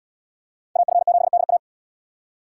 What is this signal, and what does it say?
Morse code, heard as a keyed tone, about 71 wpm, 700 Hz. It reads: AF7DA